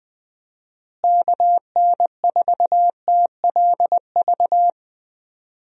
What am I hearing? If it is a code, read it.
Morse code, heard as a keyed tone, about 20 wpm, 700 Hz. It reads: KN4TLV
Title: USA Call Signs - 6 Characters - Repeated 2x - 20wpm